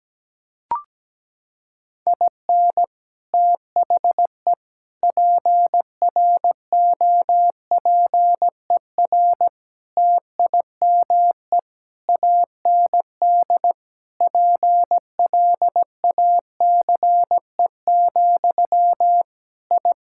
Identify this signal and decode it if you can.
Morse code, heard as a keyed tone, about 17 wpm, 700 Hz. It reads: IN THE PROPER TIME AND PLACE, I
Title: Sentences from Top 1000 Words - Review Part 5 - 17wpm